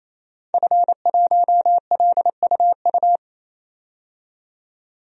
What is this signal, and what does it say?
Morse code, heard as a keyed tone, about 28 wpm, 700 Hz. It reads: F1LUU
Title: DXCC-100 Call Signs - 5 Characters - 28wpm